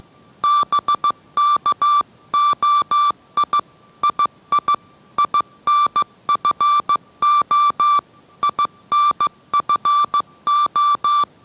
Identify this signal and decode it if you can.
Morse code, heard as a keyed tone, about 15 wpm, 1170 Hz. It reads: BKOI II INFO INFO